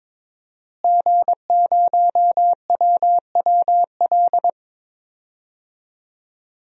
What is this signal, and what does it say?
Morse code, heard as a keyed tone, about 23 wpm, 700 Hz. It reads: G0WWL
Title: DXCC-100 Call Signs - 5 Characters - Repeated 3x - 22wpm